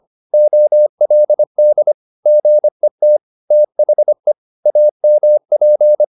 Morse code, heard as a keyed tone, about 25 wpm, 600 Hz. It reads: OLD GET THE AMP